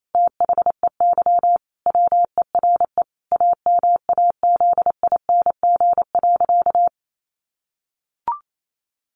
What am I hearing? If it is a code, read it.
Morse code, heard as a keyed tone, about 28 wpm, 700 Hz. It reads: THEY WERE AMAZING.
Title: Sentences from Top 1500 Words 28wpm